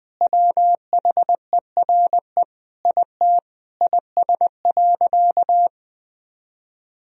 Morse code, heard as a keyed tone, about 20 wpm, 700 Hz. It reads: WHERE IT IS.
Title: Sentences from Top 1000 Words - Review Part 4 - 20wpm